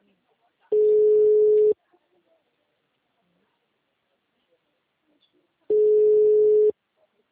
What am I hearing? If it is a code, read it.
Ringback tone (Europe, 425 Hz), with speech.